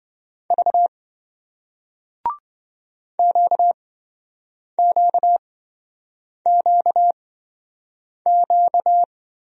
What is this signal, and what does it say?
Morse code, heard as a keyed tone, about 27 wpm, spaced as for 7 wpm, 700 Hz. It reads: V QQQQ